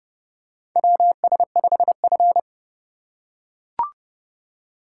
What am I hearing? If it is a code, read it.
Morse code, heard as a keyed tone, about 31 wpm, 700 Hz. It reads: WS5F
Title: USA Call Signs - 4 Characters - 4x Character Spacing - 30wpm